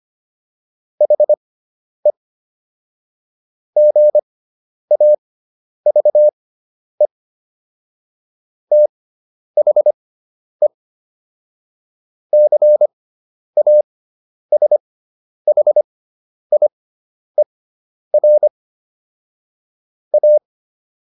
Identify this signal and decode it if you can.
Morse code, heard as a keyed tone, about 25 wpm, spaced as for 10 wpm, 600 Hz. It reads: HE GAVE THE CASHIER A